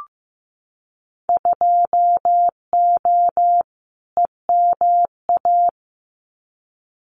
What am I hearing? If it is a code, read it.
Morse code, heard as a keyed tone, about 15 wpm, 700 Hz. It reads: 2O EMA
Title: Winter Field Day Exchanges 15wpm